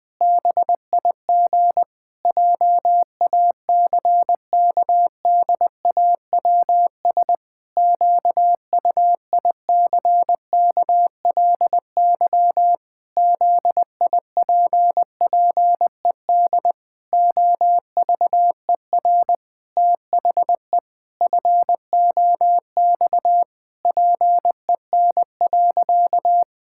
Morse code, heard as a keyed tone, about 20 wpm, 700 Hz. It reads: BIG JACKDAWS QUICKLY ZIPPED OVER THE FOX PEN.
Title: Pangram Challenge v2 - 20wpm